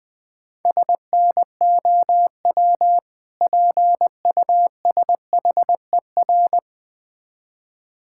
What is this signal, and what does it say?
Morse code, heard as a keyed tone, about 20 wpm, 700 Hz. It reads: SNOW PUSHER
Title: Difficult Words - Speed-Racing with Hints - 20wpm